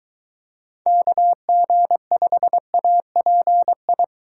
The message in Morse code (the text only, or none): KG5API